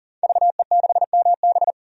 Morse code, heard as a keyed tone, about 40 wpm, 700 Hz. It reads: VE6MB
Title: DXCC-100 Call Signs - 5 Characters - 40wpm